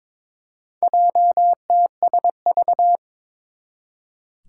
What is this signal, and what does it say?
Morse code, heard as a keyed tone, about 22 wpm, 700 Hz. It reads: JTSV
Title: Four Challenging Letter-Number Combinations 22wpm